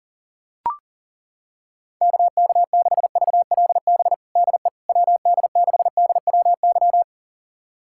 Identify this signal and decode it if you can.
Morse code, heard as a keyed tone, about 40 wpm, 700 Hz. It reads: KK6VLB DE WD6DWY
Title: USA Callsign DE Callsign - 6 Characters - 40wpm